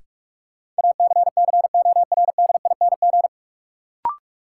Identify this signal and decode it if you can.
Morse code, heard as a keyed tone, about 46 wpm, 700 Hz. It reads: ACCORDING
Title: Sentences from Top 700 Words - New Word Drill - 45wpm